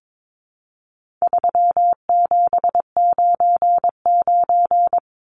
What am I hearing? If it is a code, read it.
Morse code, heard as a keyed tone, about 22 wpm, 700 Hz. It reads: 3799
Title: Numbers - 4-Digit 22wpm